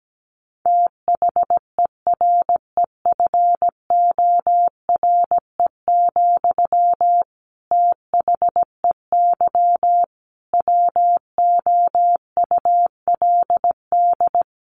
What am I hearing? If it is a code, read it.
Morse code, heard as a keyed tone, about 17 wpm, 700 Hz. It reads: THEREFORE, THEY WOULD
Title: Sentences from Top 800 Words 17wpm